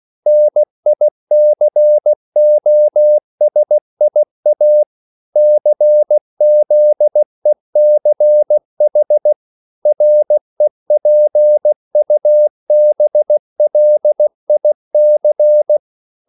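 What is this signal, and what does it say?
Morse code, heard as a keyed tone, about 16 wpm, 600 Hz. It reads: NICOSIA CZECH REPUBLIC
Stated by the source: CW practice of national capitals at 16 words per minute, from ON6ZQ.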